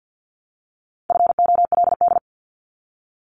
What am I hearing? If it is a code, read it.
Morse code, heard as a keyed tone, about 50 wpm, 700 Hz. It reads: FOLD
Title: Multi-letter Phonemes Repeated 3x 50wpm